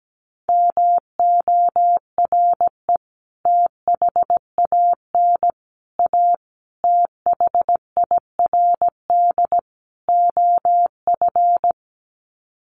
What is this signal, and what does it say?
Morse code, heard as a keyed tone, about 17 wpm, 700 Hz. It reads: MORE THAN A THIRD OF
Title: Sets of 5 Words Encore2 17wpm